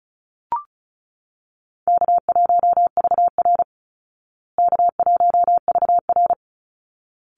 Morse code, heard as a keyed tone, about 34 wpm, 700 Hz. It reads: K1VR K1VR
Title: Easy Call Signs Repeated 2x - 35wpm